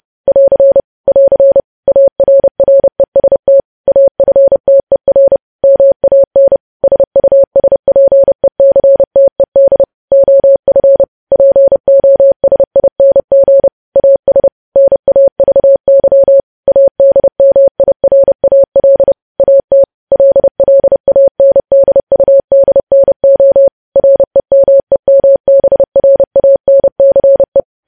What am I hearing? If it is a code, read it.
Morse code, heard as a keyed tone, about 30 wpm, 550 Hz. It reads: <AR> <AR> ARREST AFTER MAN SUSPECTED OF POSING AS NAVY ADMIRAL AT LLANDUDNO REMEMBRANCE
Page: News Headlines in Morse Code at 30 WPM